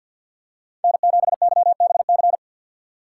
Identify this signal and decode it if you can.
Morse code, heard as a keyed tone, about 48 wpm, 700 Hz. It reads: N7YBC
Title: USA Call Signs - 5 Characters - 6x Character Spacing - 50wpm